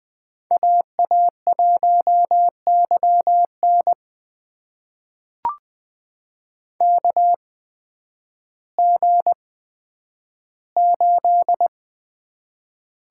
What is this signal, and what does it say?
Morse code, heard as a keyed tone, about 20 wpm, spaced as for 11 wpm, 700 Hz. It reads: AA1YN KG8
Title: USA Call Signs - 5 Characters - 8x Character Spacing - 20wpm